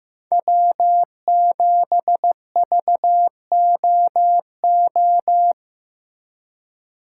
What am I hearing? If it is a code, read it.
Morse code, heard as a keyed tone, about 18 wpm, 700 Hz. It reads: W7VOO
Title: Speed Racing - USA 1x3 Call Signs 15wpm